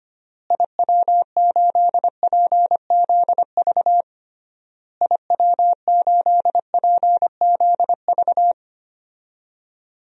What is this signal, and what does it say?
Morse code, heard as a keyed tone, about 25 wpm, 700 Hz. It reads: IW8PZV IW8PZV
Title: DXCC-100 Call Signs - 6 Characters - Repeated 3x - 25wpm